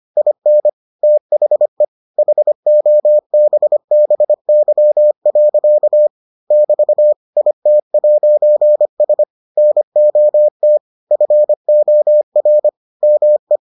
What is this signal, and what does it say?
Morse code, heard as a keyed tone, about 25 wpm, 600 Hz. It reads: IN THE HOBBY. <BT> IT'S NOT FOR ME